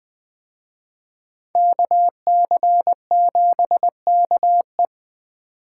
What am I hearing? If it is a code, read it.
Morse code, heard as a keyed tone, about 20 wpm, 700 Hz. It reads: KC7KE
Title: Speed Racing - USA 2x2 Call Signs 17wpm